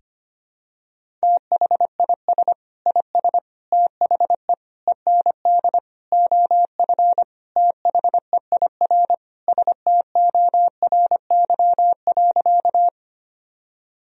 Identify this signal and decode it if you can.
Morse code, heard as a keyed tone, about 25 wpm, 700 Hz. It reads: THIS IS THE END OF THEIR STORY.
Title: Sentences from Top 1000 Words - Review Part 1 - 25wpm